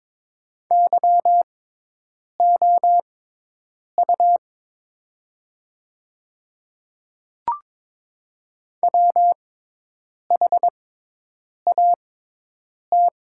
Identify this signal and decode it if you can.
Morse code, heard as a keyed tone, about 22 wpm, spaced as for 8 wpm, 700 Hz. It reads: YOU WHAT
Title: 6x Character Spacing - Top 100 Words - 22wpm